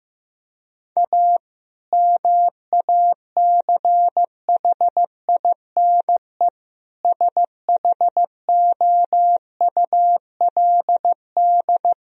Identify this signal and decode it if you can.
Morse code, heard as a keyed tone, about 15 wpm, 700 Hz. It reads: A MACHINE SHOULD
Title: Sentences from Top 1000 Words - Review Part 2 - 15wpm